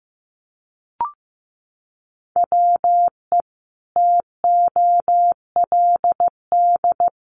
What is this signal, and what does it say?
Morse code, heard as a keyed tone, about 15 wpm, 700 Hz. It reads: WE TOLD
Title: Sentences from Top 1300 Words 15wpm